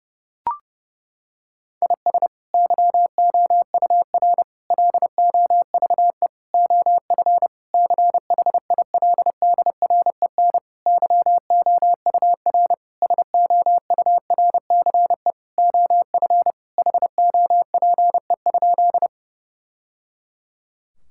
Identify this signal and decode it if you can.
Morse code, heard as a keyed tone, about 28 wpm, 700 Hz. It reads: IS YOUR LOVE OF CHILDREN YOUR SOURCE OF HOPE?
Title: Sentences from Top 1000 Words - Review Part 5 - 30wpm